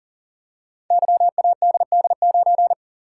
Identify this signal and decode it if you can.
Morse code, heard as a keyed tone, about 39 wpm, 700 Hz. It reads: YADD9
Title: Code Groups - 5 Letter:Numbers - 40wpm